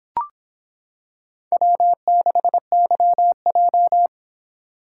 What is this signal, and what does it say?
Morse code, heard as a keyed tone, about 25 wpm, 700 Hz. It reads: W6YJ